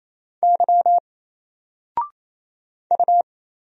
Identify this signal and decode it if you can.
Morse code, heard as a keyed tone, about 28 wpm, 700 Hz. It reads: Y U
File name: Single-Letter-Number-28wpm